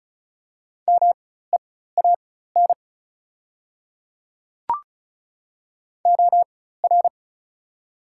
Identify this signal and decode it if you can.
Morse code, heard as a keyed tone, about 36 wpm, spaced as for 16 wpm, 700 Hz. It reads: MEAN OR
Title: 4x Character Spacing - Top 200 Words - 35wpm